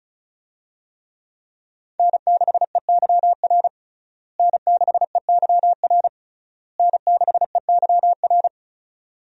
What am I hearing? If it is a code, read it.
Morse code, heard as a keyed tone, about 36 wpm, 700 Hz. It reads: N6EYR N6EYR N6EYR
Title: USA Call Signs - 5 Characters - Repeated 3x - 35wpm